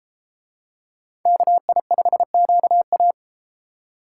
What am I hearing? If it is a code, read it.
Morse code, heard as a keyed tone, about 32 wpm, 700 Hz. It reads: KI5QA